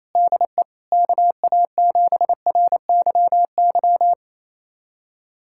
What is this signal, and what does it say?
Morse code, heard as a keyed tone, about 28 wpm, 700 Hz. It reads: DE KA7RYY
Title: USA Callsign DE Callsign - 6 Characters - 28wpm